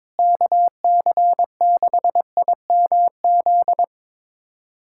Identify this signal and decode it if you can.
Morse code, heard as a keyed tone, about 21 wpm, 700 Hz. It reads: KC6IMZ
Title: USA Callsign DE Callsign 22wpm